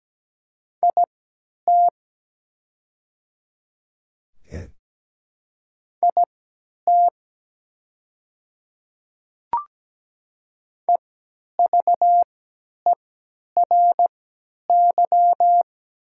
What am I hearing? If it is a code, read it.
Morse code, heard as a keyed tone, about 17 wpm, spaced as for 10 wpm, 700 Hz. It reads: IT IT EVERY
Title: 3x Character Spacing - Top 200 Words - 17wpm